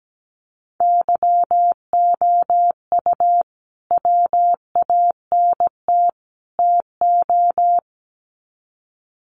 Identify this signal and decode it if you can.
Morse code, heard as a keyed tone, about 17 wpm, 700 Hz. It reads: YOU WANT TO
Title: Sets of 3 Words 17wpm